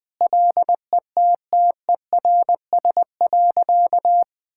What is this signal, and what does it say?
Morse code, heard as a keyed tone, about 20 wpm, 700 Hz. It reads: LETTERS.
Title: Sentences from Top 1000 Words - Review Part 6 - 20wpm